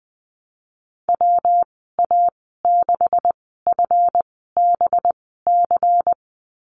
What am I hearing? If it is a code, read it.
Morse code, heard as a keyed tone, about 20 wpm, spaced as for 14 wpm, 700 Hz. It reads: WA6FBC
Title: 200 Lesson 32g - Letter k - Call Signs 20wpm